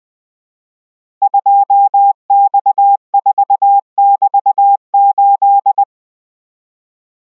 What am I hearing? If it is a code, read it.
Morse code, heard as a keyed tone, about 20 wpm, 800 Hz. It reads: 2X4<BT>8